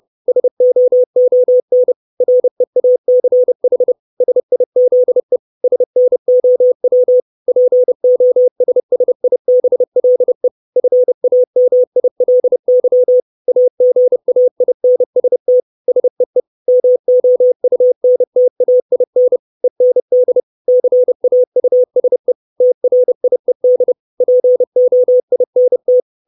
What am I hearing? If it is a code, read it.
Morse code, heard as a keyed tone, about 30 wpm, 500 Hz. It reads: SOON REACH SIZE SNOW POSSIBLE FAMILY AGAINST SEE MOUNTAIN END CAUSE TRIED POINT
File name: 632 Words 30WPM